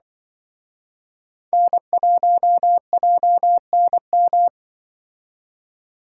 Morse code, heard as a keyed tone, about 24 wpm, 700 Hz. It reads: N1JNM
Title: Speed Racing - USA Call Signs 20wpm